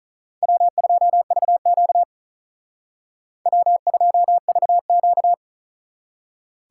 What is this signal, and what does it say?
Morse code, heard as a keyed tone, about 45 wpm, 700 Hz. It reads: W2VQ W2VQ